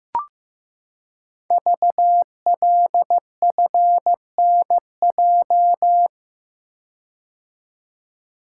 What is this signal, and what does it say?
Morse code, heard as a keyed tone, about 15 wpm, 700 Hz. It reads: VLFNJ
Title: Code Groups - 5 Letters - 15wpm